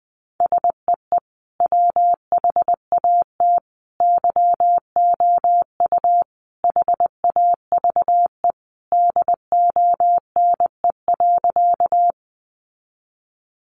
Morse code, heard as a keyed tone, about 20 wpm, 700 Hz. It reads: SEE WHAT YOU HAVE DONE.
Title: Sentences from Top 500 Words 20wpm